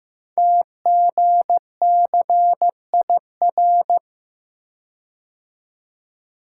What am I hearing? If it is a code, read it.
Morse code, heard as a keyed tone, about 15 wpm, 700 Hz. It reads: TGCIR
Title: Code Groups - 5 Letters - 15wpm